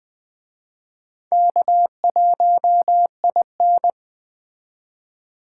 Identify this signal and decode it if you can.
Morse code, heard as a keyed tone, about 21 wpm, 700 Hz. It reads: K1IN